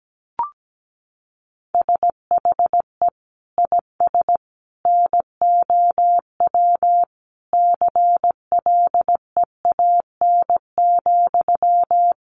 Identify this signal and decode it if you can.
Morse code, heard as a keyed tone, about 17 wpm, 700 Hz. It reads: SHE IS NOW CLEAN,
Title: Sentences from Top 1100 Words 17wpm